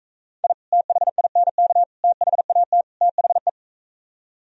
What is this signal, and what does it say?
Morse code, heard as a keyed tone, about 45 wpm, 700 Hz. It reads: I THINK THAT THE